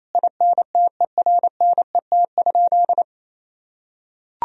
Morse code, heard as a keyed tone, about 28 wpm, 700 Hz. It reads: INTERNET?
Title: Sentences from Top 1000 Words - Review Part 4 - 28wpm